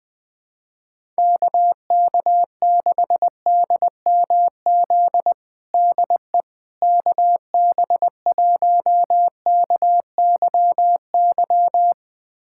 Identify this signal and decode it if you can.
Morse code, heard as a keyed tone, about 20 wpm, 700 Hz. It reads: KK6DMZ DE KB1KYY
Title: USA Callsign DE Callsign - 6 Characters - 20wpm